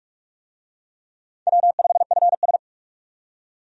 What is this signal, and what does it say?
Morse code, heard as a keyed tone, about 46 wpm, 700 Hz. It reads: W5FS
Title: USA Call Signs - 4 Characters - 45wpm